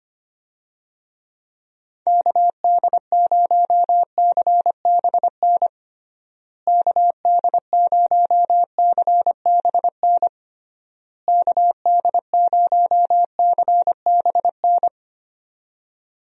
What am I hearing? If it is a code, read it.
Morse code, heard as a keyed tone, about 25 wpm, 700 Hz. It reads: KD0CBN KD0CBN KD0CBN
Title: USA Call Signs - 6 Characters - Repeated 3x - 25wpm